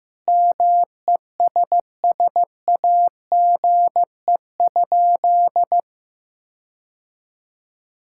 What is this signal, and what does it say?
Morse code, heard as a keyed tone, about 15 wpm, 700 Hz. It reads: MESSAGE?
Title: Sentences from Top 1600 Words 15wpm